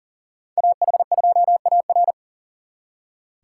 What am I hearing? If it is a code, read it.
Morse code, heard as a keyed tone, about 51 wpm, 700 Hz. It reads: AH2AR